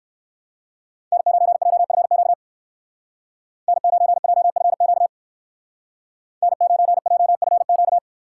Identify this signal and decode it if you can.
Morse code, heard as a keyed tone, about 54 wpm, 700 Hz. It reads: N9JFQ N9JFQ N9JFQ
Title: Speed Racing - USA 1x3 Call Signs 45wpm